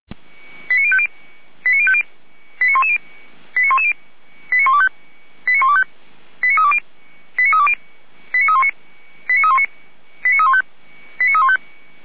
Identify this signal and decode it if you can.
Five-tone selective call (ZVEI), heard as quick runs of five tones, tones of about 70 ms. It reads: 89050 89050 89110 89110 89125 89320 89320 89319 89319 89315 89315
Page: Server at audio.f85360.feuerwehr-orlamuende.org